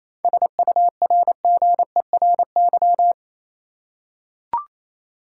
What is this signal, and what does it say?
Morse code, heard as a keyed tone, about 28 wpm, 700 Hz. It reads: SURGERY
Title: Seven Letter Words - 28wpm